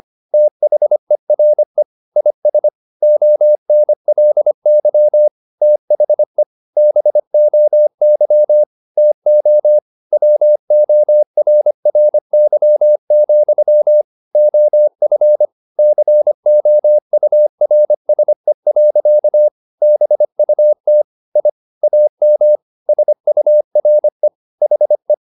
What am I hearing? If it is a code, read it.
Morse code, heard as a keyed tone, about 25 wpm, 600 Hz. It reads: THERE IS ONLY THE BOY TO WORRY, OF COURSE. BUT I AM SURE HE